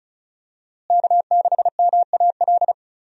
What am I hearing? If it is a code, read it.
Morse code, heard as a keyed tone, about 36 wpm, 700 Hz. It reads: K6MAL
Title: USA Call Signs - 5 Characters - Repeated 3x - 35wpm